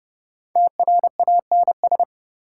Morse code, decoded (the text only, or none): TRANS